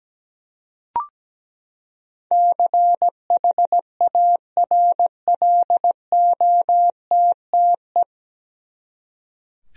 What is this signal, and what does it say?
Morse code, heard as a keyed tone, about 17 wpm, 700 Hz. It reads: CHARLOTTE
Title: USA Common Names - 17wpm